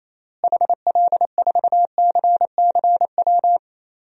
Morse code, heard as a keyed tone, about 28 wpm, 700 Hz. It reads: HL4CCW
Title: DXCC-100 Call Signs - 6 Characters - Repeated 3x - 28wpm